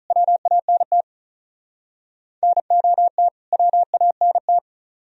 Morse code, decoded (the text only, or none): WANT NOT WANT